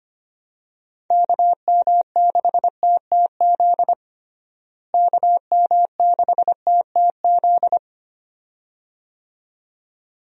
Morse code, heard as a keyed tone, about 25 wpm, 700 Hz. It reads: KM6TTZ KM6TTZ
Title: USA Call Signs - 6 Characters - Repeated 2x - 25wpm